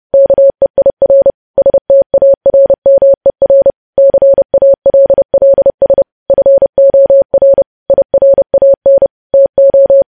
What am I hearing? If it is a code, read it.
Morse code, heard as a keyed tone, about 30 wpm, 550 Hz. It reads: KEIR STARMER CALLS FOR IRAN TO
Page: news headlines in morse code at 30 wpm